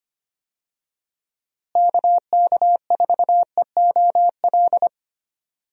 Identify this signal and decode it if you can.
Morse code, heard as a keyed tone, about 25 wpm, 700 Hz. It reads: KK4EOL
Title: USA Call Signs - 6 Characters - Repeated 3x - 25wpm